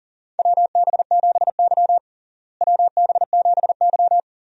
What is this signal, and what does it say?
Morse code, heard as a keyed tone, about 37 wpm, 700 Hz. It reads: WB7Y WB7Y
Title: USA Call Signs - 4 Characters - Repeated 2x - 40wpm